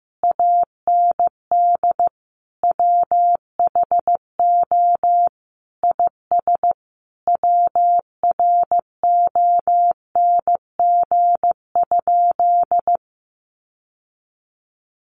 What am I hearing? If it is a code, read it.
Morse code, heard as a keyed tone, about 15 wpm, 700 Hz. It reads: AND WHO IS WRONG?
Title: Sentences from Top 900 Words 15wpm